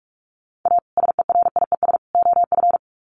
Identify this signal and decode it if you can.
Morse code, heard as a keyed tone, about 44 wpm, 700 Hz. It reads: A SERIES OF
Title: Sets of 3 Words 45wpm